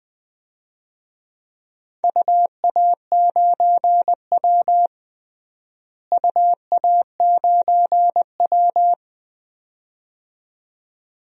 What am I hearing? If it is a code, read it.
Morse code, heard as a keyed tone, about 20 wpm, 700 Hz. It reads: UA9W UA9W
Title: DXCC-100 Call Signs - 4 Characters - Repeated 2x - 20wpm